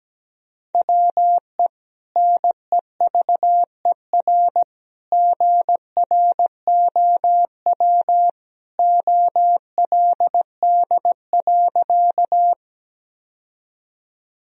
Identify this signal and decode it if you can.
Morse code, heard as a keyed tone, about 17 wpm, 700 Hz. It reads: WE NEVER GROW OLD.
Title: Sentences from Top 1000 Words - Review Part 1 - 17wpm